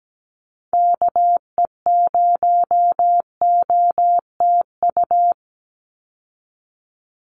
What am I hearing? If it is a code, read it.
Morse code, heard as a keyed tone, about 17 wpm, 700 Hz. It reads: KE0OTU
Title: USA Call Signs 17wpm